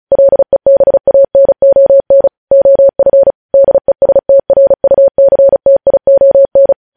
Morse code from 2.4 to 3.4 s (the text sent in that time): OF